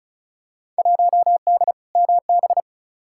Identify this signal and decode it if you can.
Morse code, heard as a keyed tone, about 36 wpm, 700 Hz. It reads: 1D MB
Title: Field Day Exchanges Encore - 35wpm